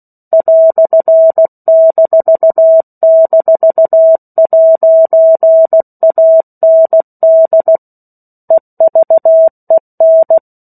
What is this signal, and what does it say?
Morse code, heard as a keyed tone, about 16 wpm, 650 Hz. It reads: "--'AND EVEN
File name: 03-Chapter-03 16 WPM